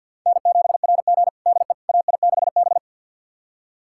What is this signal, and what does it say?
Morse code, heard as a keyed tone, about 51 wpm, 700 Hz. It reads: N7RG DE AI6B
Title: USA Callsign DE Callsign - 4 Characters - 50wpm